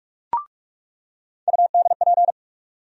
Morse code, heard as a keyed tone, about 46 wpm, 700 Hz. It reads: UDP
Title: USA Call Sign Postfixes - 3 Letters 45wpm